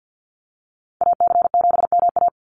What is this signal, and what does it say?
Morse code, heard as a keyed tone, about 50 wpm, 700 Hz. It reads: AC7MA
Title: NA SOTA Activator Callsigns 50wpm